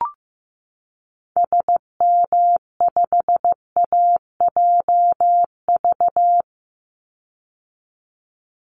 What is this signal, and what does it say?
Morse code, heard as a keyed tone, about 15 wpm, 700 Hz. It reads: SM5AJV
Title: Call Signs Repeated 3x - 15wpm